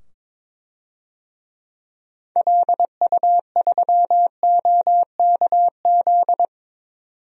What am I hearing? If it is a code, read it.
Morse code, heard as a keyed tone, about 22 wpm, 700 Hz. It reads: LU3OKZ
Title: DXCC-100 Call Signs - 6 Characters - 22wpm